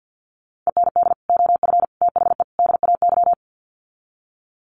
Morse code, heard as a keyed tone, about 50 wpm, 700 Hz. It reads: END OF THE DAY
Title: Sets of 4 Words 50wpm